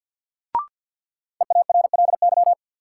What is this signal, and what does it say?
Morse code, heard as a keyed tone, about 50 wpm, 700 Hz. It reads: EARLY
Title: Sets of 3 Words - New Word Drill - 50wpm